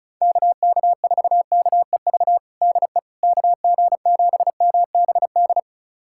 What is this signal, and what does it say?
Morse code, heard as a keyed tone, about 36 wpm, 700 Hz. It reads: KK4KEV DE KG7MBD
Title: USA Callsign DE Callsign - 6 Characters - 35wpm